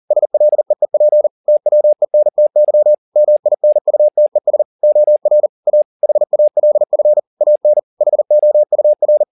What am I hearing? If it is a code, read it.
Morse code, heard as a keyed tone, about 40 wpm, 600 Hz. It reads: SLEEP TWENTY MINUTES OR A HALF AN HOUR
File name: the old man and the sea 2 40wpm